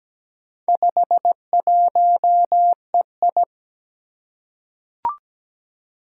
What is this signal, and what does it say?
Morse code, heard as a keyed tone, about 17 wpm, 700 Hz. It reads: 51EI
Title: Four Challenging Letter-Number Combinations 17wpm